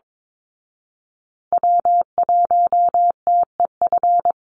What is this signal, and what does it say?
Morse code, heard as a keyed tone, about 22 wpm, 700 Hz. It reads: W1TEF
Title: Call Signs Repeated 2x - 22wpm